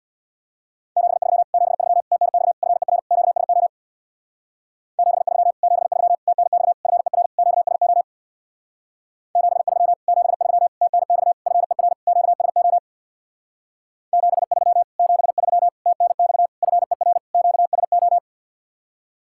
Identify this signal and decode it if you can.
Morse code, heard as a keyed tone, about 61 wpm, 700 Hz. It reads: 73 73 TNX FER QSO 73 73 TNX FER QSO 73 73 TNX FER QSO 73 73 TNX FER QSO
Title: Speed-Racing Common QSO Phrases - 50wpm